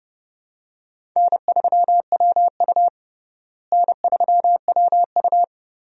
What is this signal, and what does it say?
Morse code, heard as a keyed tone, about 31 wpm, 700 Hz. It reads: N3WU N3WU
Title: USA Call Signs - 4 Characters - Repeated 2x - 30wpm